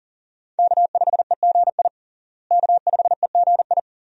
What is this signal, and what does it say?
Morse code, heard as a keyed tone, about 40 wpm, 700 Hz. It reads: K5EGI K5EGI
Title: USA Call Signs - 5 Characters - Repeated 2x - 40wpm